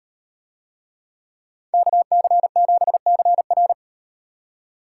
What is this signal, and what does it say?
Morse code, heard as a keyed tone, about 37 wpm, 700 Hz. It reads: KC7CR